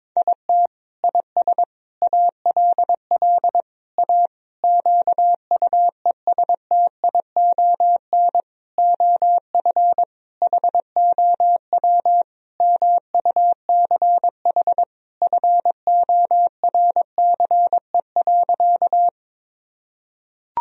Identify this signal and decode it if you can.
Morse code, heard as a keyed tone, about 22 wpm, 700 Hz. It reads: IT IS ALL A QUESTION OF HOW MUCH FORCE.
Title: Sentences from Top 1000 Words - Review Part 3 - 22wpm